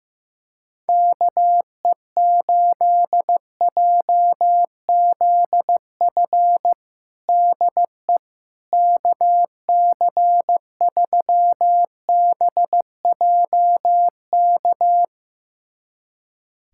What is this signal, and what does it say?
Morse code, heard as a keyed tone, about 15 wpm, 700 Hz. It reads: KE8JZF DE KC3BJK
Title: USA Callsign DE Callsign - 6 Characters - 15wpm